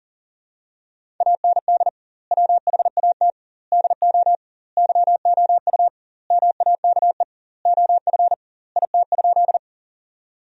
Morse code, heard as a keyed tone, about 40 wpm, 700 Hz. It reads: AND WHAT DO YOU MAKE OF IT?
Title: 2x Word-Spacing - Sentences from Top 100 Words - 40wpm